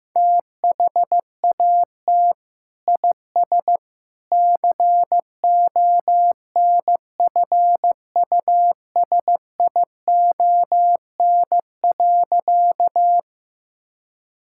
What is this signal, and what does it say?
Morse code, heard as a keyed tone, about 15 wpm, 700 Hz. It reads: THAT IS CONFUSION.
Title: Sentences from Top 1600 Words 15wpm